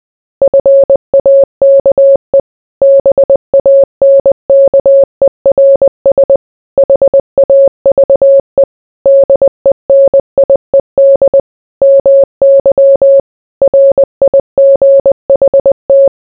Morse code, heard as a keyed tone, about 20 wpm, 550 Hz. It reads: FAKE BANKERS HAVE DENIED MY RIGHT